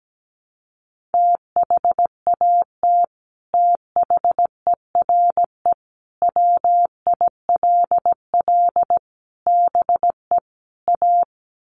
Morse code, heard as a keyed tone, about 17 wpm, 700 Hz. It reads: THAT THERE WILL BE A
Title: Sets of 5 Words Encore2 17wpm